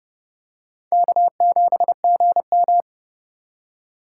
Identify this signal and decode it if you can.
Morse code, heard as a keyed tone, about 28 wpm, 700 Hz. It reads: K7GM